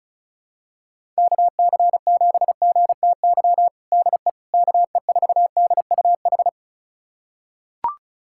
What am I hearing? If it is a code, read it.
Morse code, heard as a keyed tone, about 34 wpm, 700 Hz. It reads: KC7GTY DE KE4DUH
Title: USA Callsign DE Callsign - 6 Characters - 35wpm